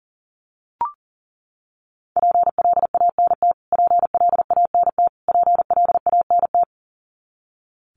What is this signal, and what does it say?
Morse code, heard as a keyed tone, about 40 wpm, 700 Hz. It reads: PLANT PLANT PLANT
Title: Top 300 Words Repeated 40wpm